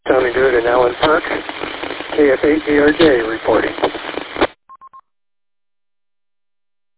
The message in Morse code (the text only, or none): none